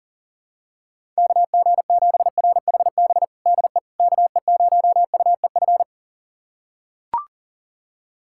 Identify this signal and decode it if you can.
Morse code, heard as a keyed tone, about 39 wpm, 700 Hz. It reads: KG7RHB DE KE0UEF
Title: USA Callsign DE Callsign - 6 Characters - 40wpm